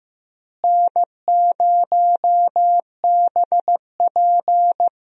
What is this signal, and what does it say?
Morse code, heard as a keyed tone, about 15 wpm, 700 Hz. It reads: N0BP
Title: USA Call Signs - 4 Characters - Repeated 3x - 15wpm